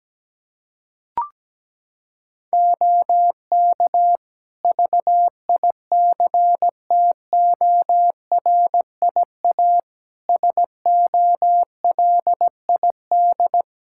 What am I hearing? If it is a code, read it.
Morse code, heard as a keyed tone, about 17 wpm, 700 Hz. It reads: OK VICTORIA SOLID
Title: Practice QSO - Exchange 3 - More Info - 17wpm